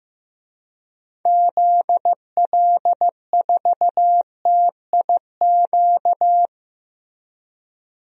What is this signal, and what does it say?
Morse code, heard as a keyed tone, about 15 wpm, 700 Hz. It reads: ZL4TIQ
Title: DXCC-100 Call Signs - 6 Characters - Repeated 3x - 15wpm